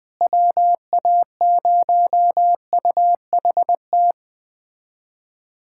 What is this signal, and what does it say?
Morse code, heard as a keyed tone, about 20 wpm, 700 Hz. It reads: WA0UHT
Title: USA Callsign DE Callsign - 6 Characters - 20wpm